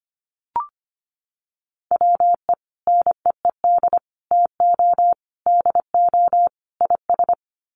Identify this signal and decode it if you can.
Morse code, heard as a keyed tone, about 25 wpm, 700 Hz. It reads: WE NEED TO DO IS
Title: Sets of 5 Words Encore2 25wpm